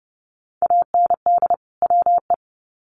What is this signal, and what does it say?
Morse code, heard as a keyed tone, about 30 wpm, 700 Hz. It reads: AND WE
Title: Sets of 2 Words 30wpm